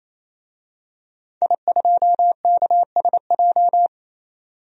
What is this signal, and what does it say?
Morse code, heard as a keyed tone, about 28 wpm, 700 Hz. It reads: I2KSJ
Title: DXCC-100 Call Signs - 5 Characters - 28wpm